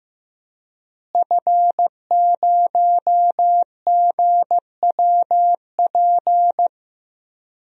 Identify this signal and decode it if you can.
Morse code, heard as a keyed tone, about 15 wpm, 700 Hz. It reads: F0GWP
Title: DXCC-100 Call Signs - 5 Characters - Repeated 2x - 15wpm